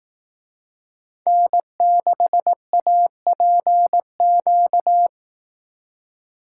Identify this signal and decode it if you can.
Morse code, heard as a keyed tone, about 18 wpm, 700 Hz. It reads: N6APQ
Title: Speed Racing - USA 1x3 Call Signs 15wpm